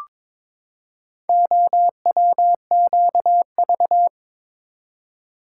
Morse code, heard as a keyed tone, about 22 wpm, 700 Hz. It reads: OWQV
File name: practice / 22wpm / Four Letter-Number Combinations - 22wpm